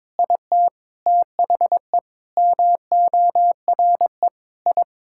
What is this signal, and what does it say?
Morse code, heard as a keyed tone, about 22 wpm, 700 Hz. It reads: IT THE MORE I
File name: Sets of 9 Words - 22wpm